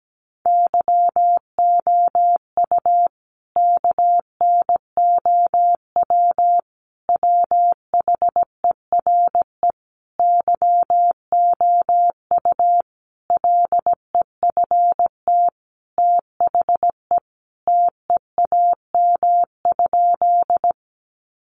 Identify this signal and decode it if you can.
Morse code, heard as a keyed tone, about 17 wpm, 700 Hz. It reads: YOU KNOW WHERE YOU LEFT THE TEAM?
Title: Sentences from Top 1500 Words 17wpm